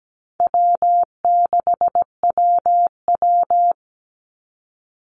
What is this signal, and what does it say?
Morse code, heard as a keyed tone, about 17 wpm, 700 Hz. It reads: W6WW
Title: NA SOTA Chaser Callsigns 17wpm